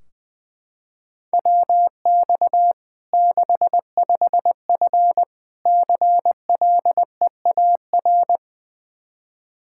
Morse code, heard as a keyed tone, about 20 wpm, 700 Hz. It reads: WX 65F CLEAR
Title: Practice QSO - Exchange 2 - Info - 20wpm